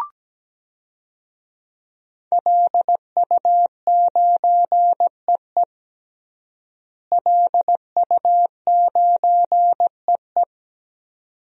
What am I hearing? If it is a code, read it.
Morse code, heard as a keyed tone, about 17 wpm, 700 Hz. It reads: LU9EE LU9EE
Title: DXCC-100 Call Signs - 5 Characters - Repeated 2x - 17wpm